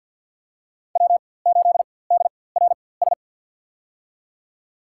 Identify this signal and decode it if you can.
Morse code, heard as a keyed tone, about 53 wpm, spaced as for 24 wpm, 700 Hz. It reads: W8DRS
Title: USA Call Signs - 4x Character Spacing - 50wpm